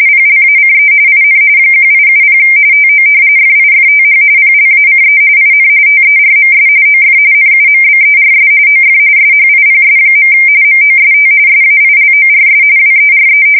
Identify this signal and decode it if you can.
RTTY, about 45.45 baud. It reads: PLACE IN 1870 TO FIFTH PLACE IN TERMS OF NUMBERS OF SHIPS. MOST SAILORS WERE
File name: fsk170 / usnavy1